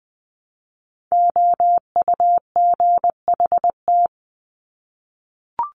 Morse code, heard as a keyed tone, about 20 wpm, 700 Hz. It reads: OUGHT
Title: Top 1751-2000 Words 20wpm